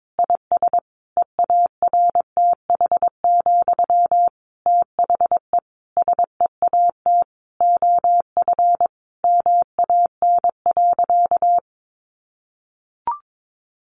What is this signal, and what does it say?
Morse code, heard as a keyed tone, about 22 wpm, 700 Hz. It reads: IS EARTH, THE SEAT OF MAN.
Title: Sentences from Top 1500 Words 22wpm